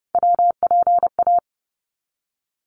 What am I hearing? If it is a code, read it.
Morse code, heard as a keyed tone, about 30 wpm, 700 Hz. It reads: WPA
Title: ARRL Sections 30wpm